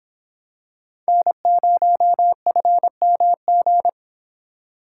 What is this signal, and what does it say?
Morse code, heard as a keyed tone, about 25 wpm, 700 Hz. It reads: N0FMG